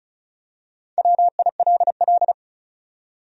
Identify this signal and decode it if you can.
Morse code, heard as a keyed tone, about 34 wpm, 700 Hz. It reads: WILL
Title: USA Common Names - 35wpm